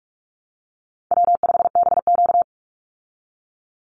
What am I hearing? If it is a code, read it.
Morse code, heard as a keyed tone, about 44 wpm, 700 Hz. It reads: W5BQ
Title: Call Signs Repeated 3x - 45wpm